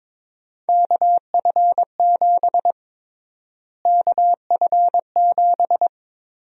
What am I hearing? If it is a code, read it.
Morse code, heard as a keyed tone, about 22 wpm, 700 Hz. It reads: KF7 KF7
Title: USA Call Sign Prefixes - Repeated 2x - 22wpm